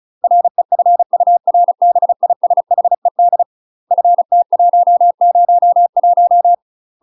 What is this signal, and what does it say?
Morse code, heard as a keyed tone, about 34 wpm, 700 Hz. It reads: REFURBISHED FT101
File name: UK MASTER QSOs TRAINING / 35WPM / UK MASTER QSOs 0036